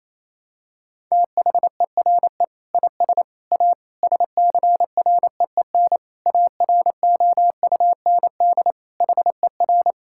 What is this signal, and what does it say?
Morse code, heard as a keyed tone, about 27 wpm, 700 Hz. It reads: THERE IS A SCREEN AROUND HER
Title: Sentences from Top 1000 Words - Review Part 6 - 28wpm